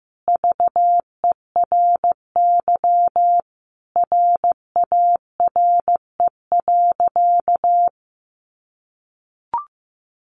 Morse code, heard as a keyed tone, about 15 wpm, 700 Hz. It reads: VERY RARE.
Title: Sentences from Top 1100 Words 15wpm